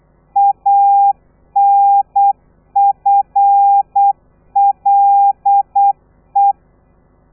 Morse code, heard as a keyed tone, about 8 wpm, 800 Hz. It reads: ANFLE
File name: telegrafia / i2tfj / Lez13 35